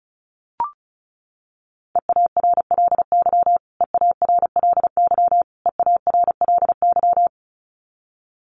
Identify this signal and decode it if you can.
Morse code, heard as a keyed tone, about 34 wpm, 700 Hz. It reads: EARLY EARLY EARLY
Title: Top 500 Words Repeated 35wpm